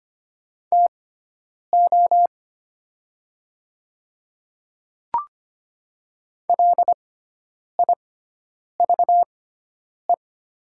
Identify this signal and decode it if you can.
Morse code, heard as a keyed tone, about 25 wpm, spaced as for 9 wpm, 700 Hz. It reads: TO LIVE
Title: 6x Character Spacing - Top 500 Words - 25wpm